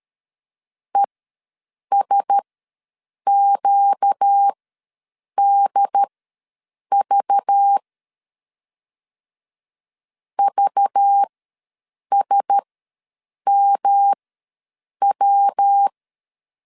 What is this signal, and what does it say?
Morse code, heard as a keyed tone, about 13 wpm, spaced as for 7 wpm, 800 Hz. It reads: ESQDV VSMW
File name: LEZIONI / 5 LBVC / 5 LBV ws 12wpm 3015